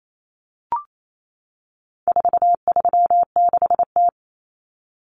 Morse code, heard as a keyed tone, about 28 wpm, 700 Hz. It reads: 436T
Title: Common Cut Numbers - 4-Digit 28wpm